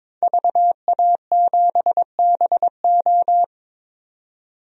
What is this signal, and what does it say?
Morse code, heard as a keyed tone, about 23 wpm, 700 Hz. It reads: VA7BO
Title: DXCC-100 Call Signs - 5 Characters - 22wpm